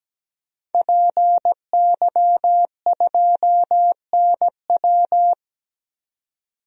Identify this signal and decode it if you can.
Morse code, heard as a keyed tone, about 17 wpm, 700 Hz. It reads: PY2NW
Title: DXCC-100 Call Signs - 5 Characters - 17wpm